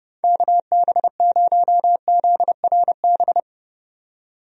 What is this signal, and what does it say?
Morse code, heard as a keyed tone, about 30 wpm, 700 Hz. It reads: KB0ZRB
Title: USA Call Signs - 6 Characters - 30wpm